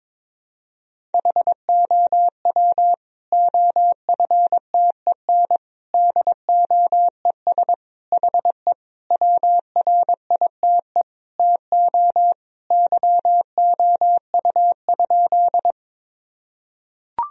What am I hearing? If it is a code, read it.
Morse code, heard as a keyed tone, about 22 wpm, 700 Hz. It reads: HOW OFTEN DOES HE WRITE TO YOU?
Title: Sentences from Top 1000 Words - Review Part 1 - 22wpm